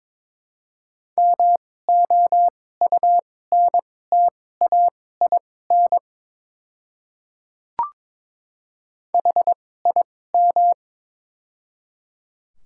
Morse code, heard as a keyed tone, about 22 wpm, spaced as for 16 wpm, 700 Hz. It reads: MOUNTAIN HIM
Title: 2x Character Spacing - Top 300 Words - 22wpm